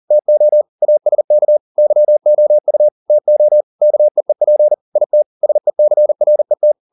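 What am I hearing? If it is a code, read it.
Morse code, heard as a keyed tone, about 40 wpm, 600 Hz. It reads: TO ASK YOU TO KEEP IT SECRET
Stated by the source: Morse code at 40 wpm, practice text, CW med Gnister.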